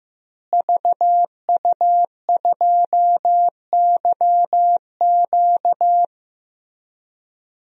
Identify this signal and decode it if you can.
Morse code, heard as a keyed tone, about 15 wpm, 700 Hz. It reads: VU2YQ
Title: DXCC-100 Call Signs - 5 Characters - Repeated 3x - 15wpm